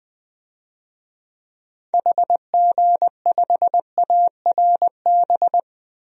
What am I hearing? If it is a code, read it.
Morse code, heard as a keyed tone, about 20 wpm, 700 Hz. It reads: HG5ARB
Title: DXCC-100 Call Signs - 6 Characters - Repeated 2x - 20wpm